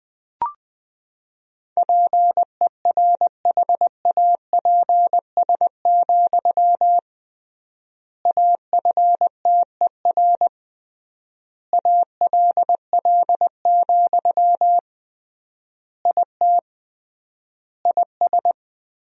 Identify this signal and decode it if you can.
Morse code, heard as a keyed tone, about 21 wpm, 700 Hz. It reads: PERHAPS, AFTER ALL, IT IS
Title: 3x Word-Spacing - Sentences from Top 500 Words - 20wpm